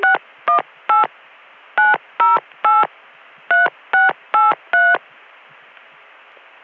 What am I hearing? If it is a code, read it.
DTMF, heard as telephone touch tones, tones of about 170 ms. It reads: 618 908 3683